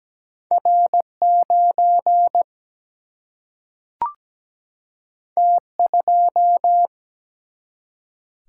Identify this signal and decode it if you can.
Morse code, heard as a keyed tone, about 17 wpm, 700 Hz. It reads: R 9 T 2
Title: Single-Letter Single-Number 17wpm